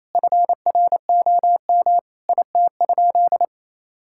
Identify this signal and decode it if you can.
Morse code, heard as a keyed tone, about 28 wpm, 700 Hz. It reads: FROM IT?
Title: Sentences from Top 1000 Words - Review Part 5 - 28wpm